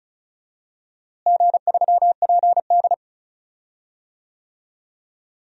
Morse code, heard as a keyed tone, about 36 wpm, 700 Hz. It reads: G3PD
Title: DXCC-100 Call Signs - 4 Characters - Repeated 2x - 35wpm